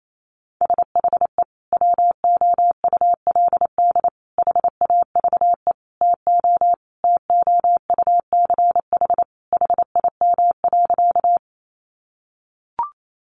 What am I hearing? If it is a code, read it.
Morse code, heard as a keyed tone, about 28 wpm, 700 Hz. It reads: SHE WOULD HAVE TO TOUCH HIM.
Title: Sentences from Top 1200 Words 28wpm